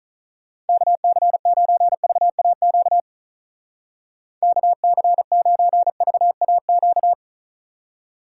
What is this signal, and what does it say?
Morse code, heard as a keyed tone, about 42 wpm, 700 Hz. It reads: KC9VAQ KC9VAQ
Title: Speed Racing - USA 2x3 Call Signs 35wpm